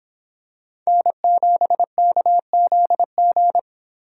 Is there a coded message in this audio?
Yes: Morse code from 0.9 to 3.6 s, heard as a keyed tone, about 26 wpm, 700 Hz. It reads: N7KZG